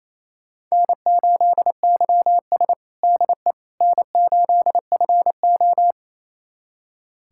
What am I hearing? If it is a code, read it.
Morse code, heard as a keyed tone, about 27 wpm, 690 Hz. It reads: N8YS DE N8FO
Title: USA Callsign DE Callsign - 4 Characters - 28wpm